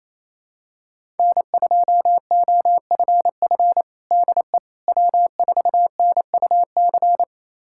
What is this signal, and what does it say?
Morse code, heard as a keyed tone, about 28 wpm, 700 Hz. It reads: N2OFF DE W4NUC
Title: USA Callsign DE Callsign - 5 Characters - 28wpm